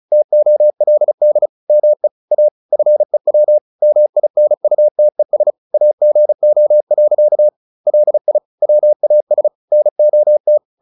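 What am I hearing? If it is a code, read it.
Morse code, heard as a keyed tone, about 34 wpm, 600 Hz. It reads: TOLD ME A FEW MINUTES AGO. LI WAS NOT
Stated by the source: Morse practice at 35 wpm from CW med Gnister, sending Ken Follett's novel Never in English.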